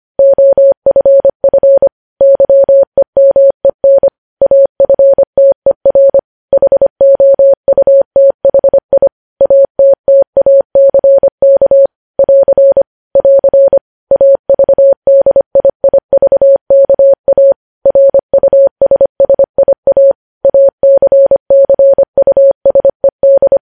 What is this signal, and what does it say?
Morse code, heard as a keyed tone, about 25 wpm, 550 Hz. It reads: OFF YEMEN AFTER HOUTHI ATTACK <AR> <AR> AVDIIVKA RUSSIA ACCUSED
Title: News Headlines at 25